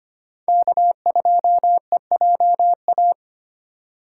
Morse code, heard as a keyed tone, about 25 wpm, 700 Hz. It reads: K2EJA
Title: USA Call Signs - 5 Characters - 4x Character Spacing - 25wpm